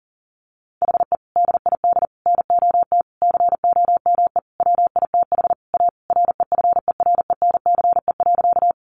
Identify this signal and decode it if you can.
Morse code, heard as a keyed tone, about 40 wpm, 700 Hz. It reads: HE DID NOT COME WITH A REFERENCE.
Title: Sentences from Top 1200 Words 40wpm